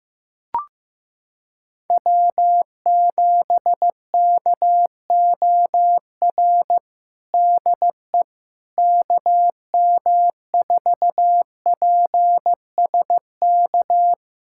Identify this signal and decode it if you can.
Morse code, heard as a keyed tone, about 15 wpm, 700 Hz. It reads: W7KOR DE KM4PSK
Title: USA Callsign DE Callsign 15wpm